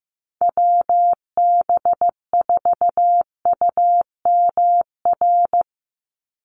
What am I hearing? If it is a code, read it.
Morse code, heard as a keyed tone, about 15 wpm, 700 Hz. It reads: WB4UMR
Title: USA Call Signs 15wpm